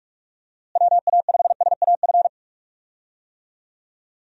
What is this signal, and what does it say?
Morse code, heard as a keyed tone, about 44 wpm, 700 Hz. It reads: WA5SAF
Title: USA Call Signs - 6 Characters - Repeated 2x - 45wpm